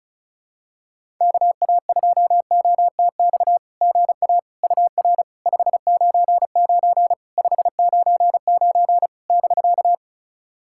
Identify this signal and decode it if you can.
Morse code, heard as a keyed tone, about 34 wpm, 700 Hz. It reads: KA2OTX GA UR 599 599 <BK>
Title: SOTA QSO Practice 35wpm